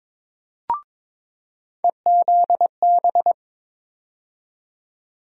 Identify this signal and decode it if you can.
Morse code, heard as a keyed tone, about 22 wpm, 700 Hz. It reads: EZB
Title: USA Call Sign Postfixes - 3 Letters 22wpm